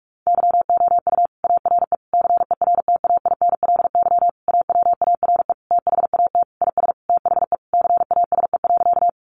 Morse code, heard as a keyed tone, about 44 wpm, 700 Hz. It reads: YOU ARE CERTAINLY AWARE THAT IS THE CASE.
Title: Sentences from Top 1200 Words 45wpm